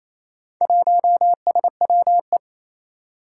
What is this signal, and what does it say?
Morse code, heard as a keyed tone, about 27 wpm, 700 Hz. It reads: 1SWE
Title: Four Challenging Letter-Number Combinations 28wpm